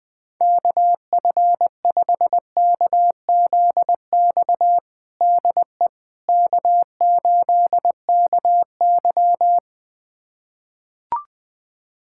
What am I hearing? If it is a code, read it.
Morse code, heard as a keyed tone, about 20 wpm, 700 Hz. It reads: KF5KZX DE K8KY
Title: USA Callsign DE Callsign 20wpm